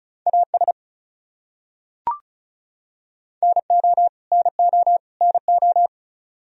Morse code, heard as a keyed tone, about 34 wpm, 700 Hz. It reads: AS NO NO NO
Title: Top 100 Words Repeated 35wpm